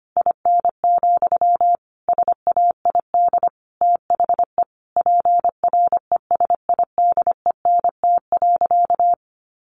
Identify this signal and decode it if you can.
Morse code, heard as a keyed tone, about 25 wpm, 700 Hz. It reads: IN, SAID THE PRESIDENT.
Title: Sentences from Top 1000 Words 25wpm